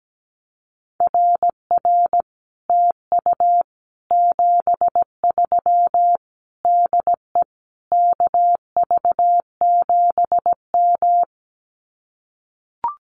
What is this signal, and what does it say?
Morse code, heard as a keyed tone, about 17 wpm, 700 Hz. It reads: RR TU 73 DE KV7M
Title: Common QSO Phrases Encore 17wpm